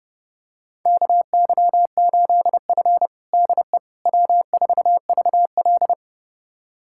Morse code, heard as a keyed tone, about 30 wpm, 700 Hz. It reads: KY8F DE W4VL
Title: USA Callsign DE Callsign - 4 Characters - 30wpm